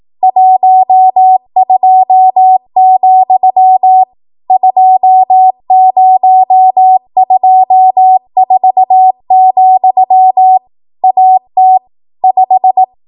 Morse code, heard as a keyed tone, about 18 wpm, 750 Hz. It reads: 12, 2024, AT 5
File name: Archive / 18 WPM / 18WPM